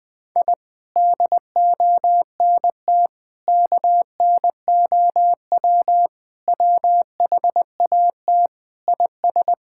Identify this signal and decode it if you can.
Morse code, heard as a keyed tone, about 21 wpm, 700 Hz. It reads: I DONT KNOW WHAT IS
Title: Sentences with 7-Word Phrases - 20wpm